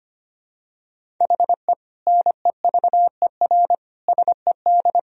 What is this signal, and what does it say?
Morse code, heard as a keyed tone, about 25 wpm, 700 Hz. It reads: HE NEVER SED
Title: Sentences from Top 500 Words and Abbreviations 25wpm